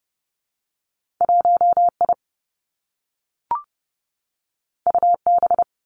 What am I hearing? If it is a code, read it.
Morse code, heard as a keyed tone, about 30 wpm, 700 Hz. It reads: 1 I U B
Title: Two Challenging Letter-Number Combinations 30wpm